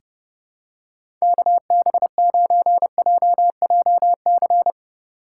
Morse code, heard as a keyed tone, about 30 wpm, 700 Hz. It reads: KB9JJC
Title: USA Call Signs - 2x Character Spacing - 30wpm